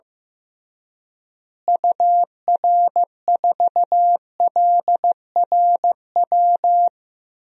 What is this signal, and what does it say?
Morse code, heard as a keyed tone, about 15 wpm, 700 Hz. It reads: UR4LRW
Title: DXCC-100 Call Signs - 6 Characters - Repeated 3x - 15wpm